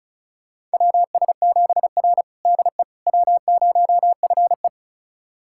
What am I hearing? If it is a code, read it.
Morse code, heard as a keyed tone, about 34 wpm, 700 Hz. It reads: WS7R DE W0FE
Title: USA Callsign DE Callsign - 4 Characters - 35wpm